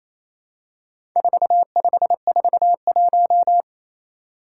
Morse code, heard as a keyed tone, about 29 wpm, 700 Hz. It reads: 4541
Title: Four Challenging Letter-Number Combinations 28wpm